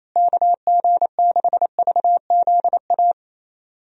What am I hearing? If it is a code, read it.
Morse code, heard as a keyed tone, about 28 wpm, 700 Hz. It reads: KG6VZA
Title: USA Call Signs - 6 Characters - Repeated 3x - 28wpm